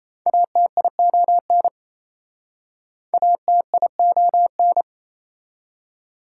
Morse code, heard as a keyed tone, about 37 wpm, 700 Hz. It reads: ATION ATION